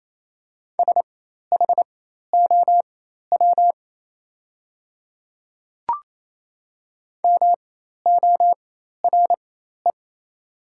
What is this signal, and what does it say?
Morse code, heard as a keyed tone, about 28 wpm, spaced as for 13 wpm, 690 Hz. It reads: SHOW MORE
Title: 4x Character Spacing - Top 200 Words - 28wpm